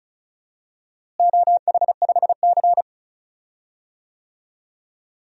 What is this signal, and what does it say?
Morse code, heard as a keyed tone, about 36 wpm, 700 Hz. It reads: OH5C
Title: DXCC-100 Call Signs - 4 Characters - Repeated 3x - 35wpm